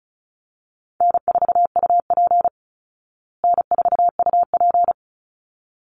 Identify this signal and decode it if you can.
Morse code, heard as a keyed tone, about 34 wpm, 700 Hz. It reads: N4UP N4UP
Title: Easy Call Signs Repeated 2x - 35wpm